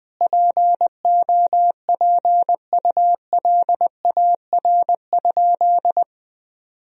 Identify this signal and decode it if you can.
Morse code, heard as a keyed tone, about 21 wpm, 700 Hz. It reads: POPULAR?
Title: Sentences from Top 1000 Words - Review Part 3 - 20wpm